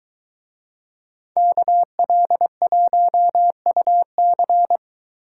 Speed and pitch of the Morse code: 23 wpm, 700 Hz

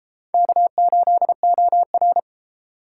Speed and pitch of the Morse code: 33 wpm, 700 Hz

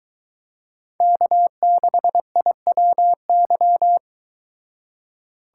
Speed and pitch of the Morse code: 22 wpm, 700 Hz

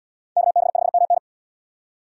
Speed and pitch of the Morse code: 77 wpm, 700 Hz

